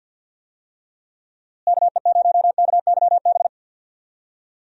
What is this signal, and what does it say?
Morse code, heard as a keyed tone, about 46 wpm, 700 Hz. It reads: KE0KYB
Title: USA Call Signs - 6 Characters - Repeated 3x - 50wpm